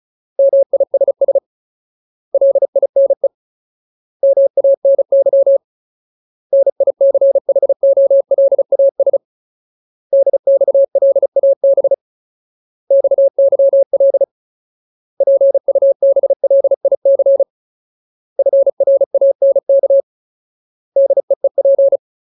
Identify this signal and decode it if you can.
Morse code, heard as a keyed tone, about 34 wpm, 550 Hz. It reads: MISS LINE MANY NICHOLAS DXLAB XYL PUBLIC FRANK DEEP
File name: Ultimate Word List / 35WPM Ultimate